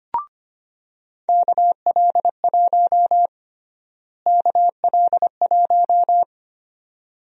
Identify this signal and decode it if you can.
Morse code, heard as a keyed tone, about 25 wpm, 700 Hz. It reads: KL1 KL1
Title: USA Call Sign Prefixes - Repeated 2x - 25wpm